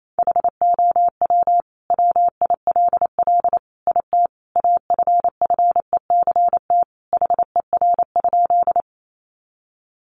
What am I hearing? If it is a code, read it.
Morse code, heard as a keyed tone, about 28 wpm, 700 Hz. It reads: HOW WILL IT AFFECT HER?
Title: Sentences from Top 900 Words 28wpm